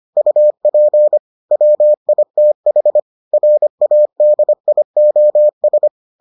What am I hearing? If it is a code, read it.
Morse code, heard as a keyed tone, about 25 wpm, 600 Hz. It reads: UP WITH RADIOS